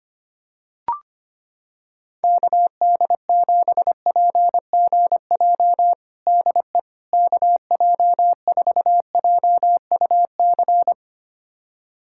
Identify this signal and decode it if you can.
Morse code, heard as a keyed tone, about 25 wpm, 700 Hz. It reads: KD7PGJ DE KJ4JUC
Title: USA Callsign DE Callsign 25wpm